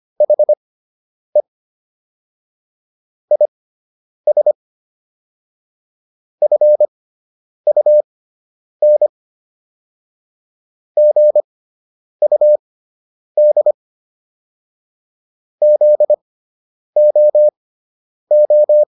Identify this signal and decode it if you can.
Morse code, heard as a keyed tone, about 25 wpm, spaced as for 9 wpm, 600 Hz. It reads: HE IS FUN GUD ZOO